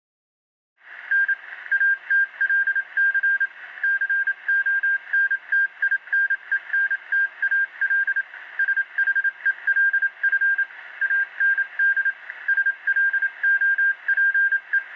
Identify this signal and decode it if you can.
Morse code, heard as a keyed tone, about 27 wpm, 1600 Hz. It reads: N ATLC CONTINENTAL SHELF AND SLOPE